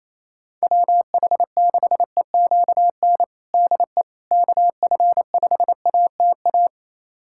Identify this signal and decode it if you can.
Morse code, heard as a keyed tone, about 28 wpm, 700 Hz. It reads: WH6EQN DE KF5ATA
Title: USA Callsign DE Callsign - 6 Characters - 28wpm